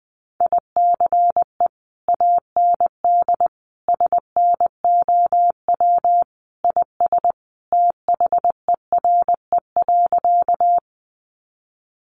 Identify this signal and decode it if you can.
Morse code, heard as a keyed tone, about 20 wpm, 700 Hz. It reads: ICE AND SNOW IS THERE.
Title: Sentences from Top 1000 Words 20wpm